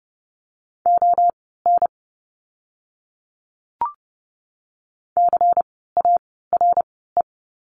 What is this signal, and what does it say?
Morse code, heard as a keyed tone, about 30 wpm, spaced as for 17 wpm, 700 Hz. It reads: ON CARE
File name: Lesson 16f - Letter c - Words 30wpm